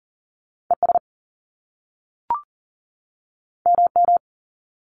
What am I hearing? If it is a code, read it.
Morse code, heard as a keyed tone, about 40 wpm, 700 Hz. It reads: ES MM
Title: Top Two Letter Combinations in Words 40wpm